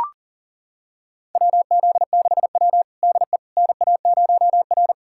Morse code, heard as a keyed tone, about 40 wpm, 700 Hz. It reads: WZ6W DE NA0R
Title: USA Callsign DE Callsign - 4 Characters - 40wpm